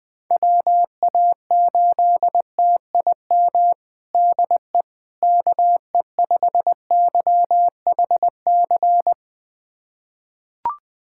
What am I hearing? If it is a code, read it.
Morse code, heard as a keyed tone, about 20 wpm, 700 Hz. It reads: WA8TIM DE KE5YHC
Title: USA Callsign DE Callsign 20wpm